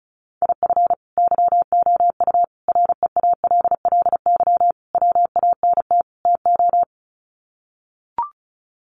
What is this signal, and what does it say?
Morse code, heard as a keyed tone, about 34 wpm, 700 Hz. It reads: IF YOU REALLY WANT TO
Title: Sets of 5 Words Encore2 35wpm